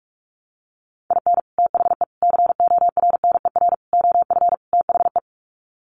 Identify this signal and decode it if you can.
Morse code, heard as a keyed tone, about 44 wpm, 700 Hz. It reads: IN THE CORNER OF THE
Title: Sets of 5 Words Encore 45wpm